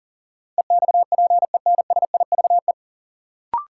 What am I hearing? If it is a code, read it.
Morse code, heard as a keyed tone, about 37 wpm, 700 Hz. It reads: EXPENSIVE
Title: Sentences from Top 1000 Words - Review Part 6 - New Word Drill - 40wpm